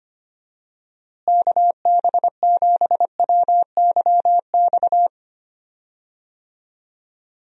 Morse code, heard as a keyed tone, about 25 wpm, 700 Hz. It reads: KB7WYX
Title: USA Call Signs - 6 Characters - Repeated 2x - 25wpm